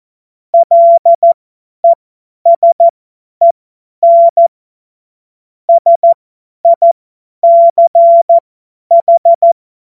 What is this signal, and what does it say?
Morse code, heard as a keyed tone, about 14 wpm, spaced as for 10 wpm, 680 Hz. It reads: LESEN SICH
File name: Morsekurs / Witze / Witz 22. Juni 2021, 50 F70